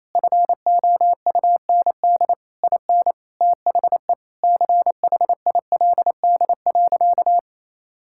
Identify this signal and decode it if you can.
Morse code, heard as a keyed tone, about 28 wpm, 700 Hz. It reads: FOUND IN THE CHILD.
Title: Sentences from Top 1000 Words - Review Part 5 - 28wpm